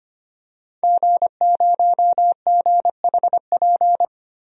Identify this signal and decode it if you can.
Morse code, heard as a keyed tone, about 25 wpm, 700 Hz. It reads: G0GHP
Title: DXCC-100 Call Signs - 5 Characters - Repeated 2x - 25wpm